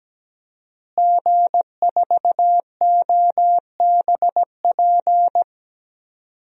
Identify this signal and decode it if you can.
Morse code, heard as a keyed tone, about 17 wpm, 700 Hz. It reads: G4OBP
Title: DXCC-100 Call Signs - 5 Characters - Repeated 2x - 17wpm